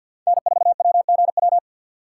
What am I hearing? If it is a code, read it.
Morse code, heard as a keyed tone, about 62 wpm, 700 Hz. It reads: N4WGW
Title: Speed Racing - USA 1x3 Call Signs 50wpm